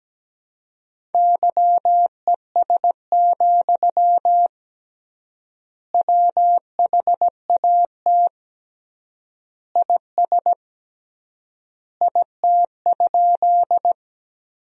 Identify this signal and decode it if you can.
Morse code, heard as a keyed tone, about 17 wpm, 700 Hz. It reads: YES, WHAT IS IT?
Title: 3x Word-Spacing - Sentences from Top 500 Words - 17wpm